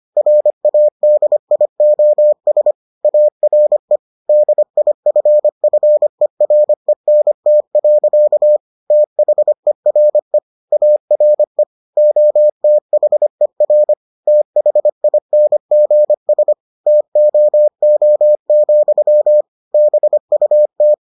Morse code, heard as a keyed tone, about 25 wpm, 600 Hz. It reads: RADIOS ARE DIFFERENT. THERE ARE OTHER THINGS TOO, BUT